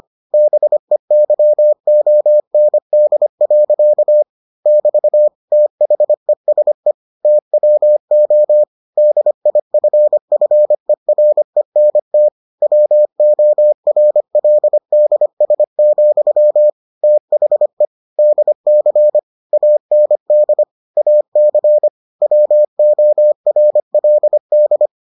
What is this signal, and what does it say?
Morse code, heard as a keyed tone, about 25 wpm, 600 Hz. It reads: BEYOND. <BT> THESE TWO DIFFERENT WORLDS, THE DC AND AC WORLD